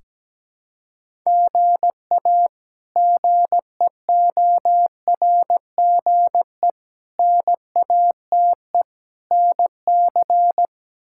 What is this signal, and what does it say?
Morse code, heard as a keyed tone, about 17 wpm, 700 Hz. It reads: GA GEORGE NATE NC
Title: K1USN SST QSO Practice - 17wpm